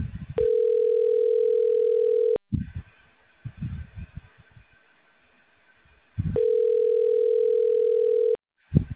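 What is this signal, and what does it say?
Ringback tone (North America), 440 + 480 Hz.
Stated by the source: Outbound telemarketing call from a business-services dialer.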